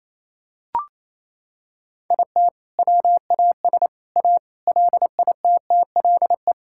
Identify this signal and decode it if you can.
Morse code, heard as a keyed tone, about 28 wpm, 700 Hz. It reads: IT WAS A LITTLE
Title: Sentences from Top 1600 Words 28wpm